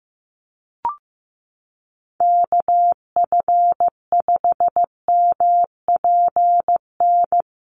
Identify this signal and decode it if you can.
Morse code, heard as a keyed tone, about 15 wpm, 700 Hz. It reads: KF5MPN
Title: USA Call Signs Repeated 2x 15wpm